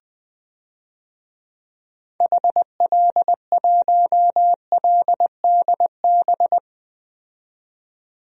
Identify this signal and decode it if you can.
Morse code, heard as a keyed tone, about 21 wpm, 700 Hz. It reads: HL1LDB
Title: DXCC-100 Call Signs - 6 Characters - 20wpm